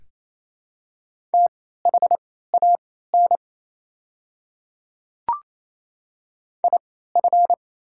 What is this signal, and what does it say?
Morse code, heard as a keyed tone, about 28 wpm, spaced as for 16 wpm, 700 Hz. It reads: THAN IF
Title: 3x Character Spacing - Top 100 Words - 28wpm